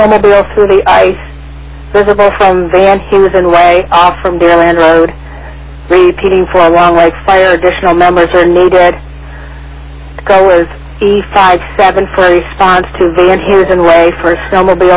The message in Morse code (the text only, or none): none